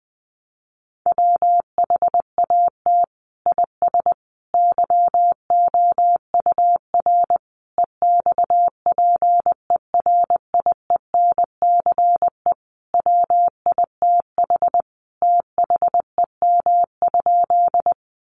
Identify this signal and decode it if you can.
Morse code, heard as a keyed tone, about 20 wpm, 700 Hz. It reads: WHAT IS YOUR EXPERIENCE WITH THEM?
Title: Sentences from Top 600 Words 20wpm